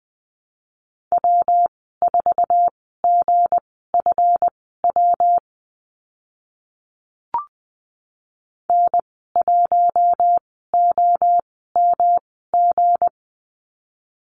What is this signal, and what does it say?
Morse code, heard as a keyed tone, about 20 wpm, spaced as for 14 wpm, 700 Hz. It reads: W4GFW N1OMG
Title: 161 Lesson 26g - Letter g - Call Signs 20wpm